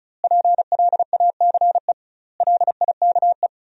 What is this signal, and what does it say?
Morse code, heard as a keyed tone, about 34 wpm, 700 Hz. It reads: PLACE LIKE
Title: 2x Word-Spacing - Sentences from Top 200 Words - 35wpm